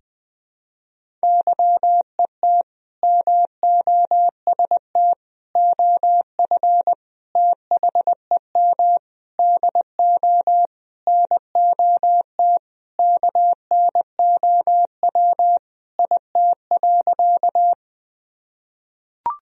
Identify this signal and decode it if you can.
Morse code, heard as a keyed tone, about 20 wpm, 700 Hz. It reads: YET MOST OF THEM DO NOT KNOW IT.
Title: Sentences from Top 1000 Words - Review Part 1 - 20wpm